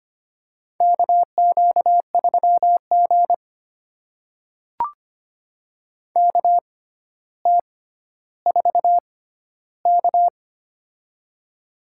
Morse code, heard as a keyed tone, about 25 wpm, spaced as for 13 wpm, 700 Hz. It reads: KQ3G KT4K
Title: USA Call Signs - 4 Characters - 6x Character Spacing - 25wpm